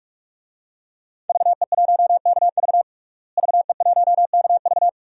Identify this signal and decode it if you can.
Morse code, heard as a keyed tone, about 46 wpm, 700 Hz. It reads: VE1KV VE1KV
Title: DXCC-100 Call Signs - 5 Characters - Repeated 2x - 45wpm